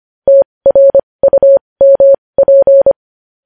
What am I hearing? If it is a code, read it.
Morse code, heard as a keyed tone, about 25 wpm, spaced as for 20 wpm, 550 Hz. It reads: TRUMP
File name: podcast / News headlines-1x-25x20